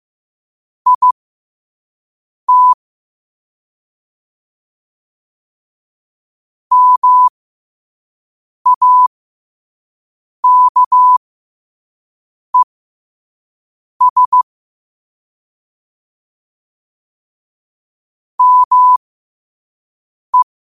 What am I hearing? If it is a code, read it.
Morse code, heard as a keyed tone, about 15 wpm, spaced as for 5 wpm, 1000 Hz. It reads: IT MAKES ME
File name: qotdfiles / 5wpm